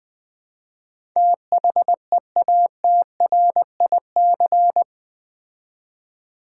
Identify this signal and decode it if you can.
Morse code, heard as a keyed tone, about 20 wpm, 700 Hz. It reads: THEATRIC
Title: Build A Word - 4 Letters At A Time - 10-Letter Words - 20wpm